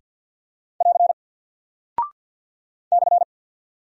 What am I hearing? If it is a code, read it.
Morse code, heard as a keyed tone, about 50 wpm, 700 Hz. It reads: <AR> /
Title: Single Character (Letter:Number:Punctuation:Prosign) 50wpm